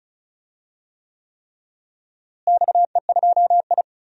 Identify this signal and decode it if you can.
Morse code, heard as a keyed tone, about 36 wpm, 700 Hz. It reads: XE2I
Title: DXCC-100 Call Signs - 4 Characters - Repeated 3x - 35wpm